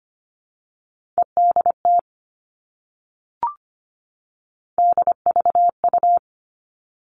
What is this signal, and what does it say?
Morse code, heard as a keyed tone, about 25 wpm, 700 Hz. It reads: EDT DVU
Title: All Cut Numbers - 3-Digit 25wpm